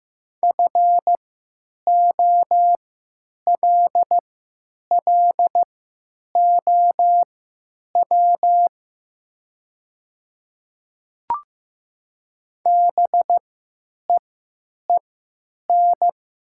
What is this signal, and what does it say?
Morse code, heard as a keyed tone, about 15 wpm, spaced as for 9 wpm, 700 Hz. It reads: FOLLOW BEEN
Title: 3x Character Spacing - Top 300 Words - 15wpm